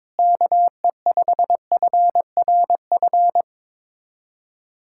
Morse code, heard as a keyed tone, about 21 wpm, 700 Hz. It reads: KE5FRF
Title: K1USN SST QSO Practice - 22wpm